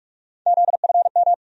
Morse code, heard as a keyed tone, about 46 wpm, 700 Hz. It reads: ZFM
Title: USA Call Sign Postfixes - 3 Letters 45wpm